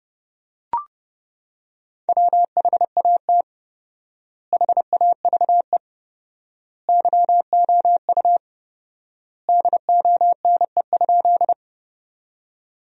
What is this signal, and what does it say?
Morse code, heard as a keyed tone, about 30 wpm, 700 Hz. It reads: WHAT HAVE YOU DONE?
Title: 4x Word-Spacing - Sentences from Top 500 Words - 30wpm